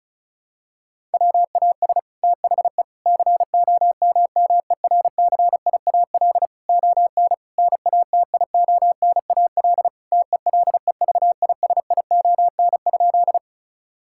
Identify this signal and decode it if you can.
Morse code, heard as a keyed tone, about 36 wpm, 700 Hz. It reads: WAS THE COMMERCIAL ON NATIONAL TELEVISION?
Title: Sentences from Top 1000 Words - Review Part 5 - 35wpm